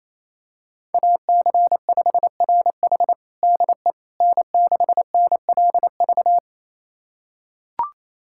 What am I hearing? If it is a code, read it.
Morse code, heard as a keyed tone, about 28 wpm, 700 Hz. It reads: AC5RH DE N6NLV
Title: USA Callsign DE Callsign 28wpm